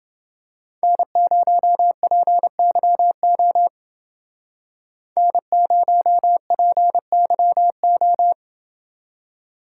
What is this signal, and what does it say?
Morse code, heard as a keyed tone, about 28 wpm, 700 Hz. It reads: N0PYO N0PYO